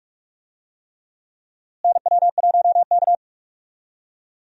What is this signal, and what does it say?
Morse code, heard as a keyed tone, about 47 wpm, 700 Hz. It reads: NW1K
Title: USA Call Signs - 4 Characters - Repeated 3x - 45wpm